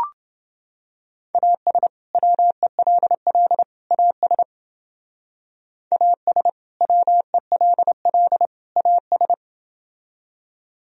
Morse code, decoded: AS WELL AS AS WELL AS

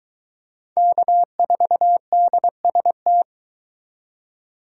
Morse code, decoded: K4DST

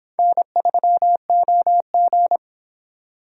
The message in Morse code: N3OG

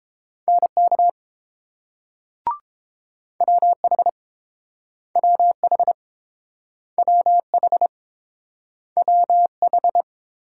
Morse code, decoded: NK WH WH WH WH

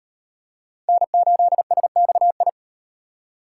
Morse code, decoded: N8SXI

